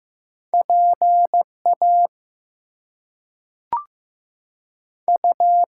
Morse code, heard as a keyed tone, about 15 wpm, 700 Hz. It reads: PA U
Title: US States and Canadian Provinces 15wpm